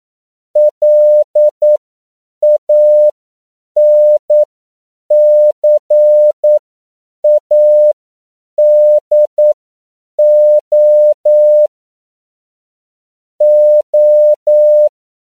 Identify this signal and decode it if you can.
Morse code, heard as a keyed tone, about 9 wpm, spaced as for 7 wpm, 600 Hz. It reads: LANCADO O
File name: CW / 10PPM